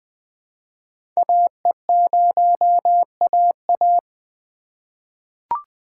Morse code, heard as a keyed tone, about 20 wpm, 700 Hz. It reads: AE0AA
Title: USA Call Signs - 5 Characters - 3x Character Spacing - 20wpm